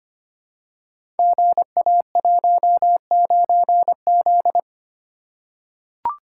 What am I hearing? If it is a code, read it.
Morse code, heard as a keyed tone, about 25 wpm, 700 Hz. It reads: GA19Z
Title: Code Groups - 5 Letter:Numbers - 25wpm